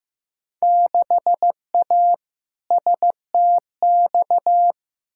Morse code, heard as a keyed tone, about 15 wpm, 700 Hz. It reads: 6A STX
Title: Field Day Exchanges - 15wpm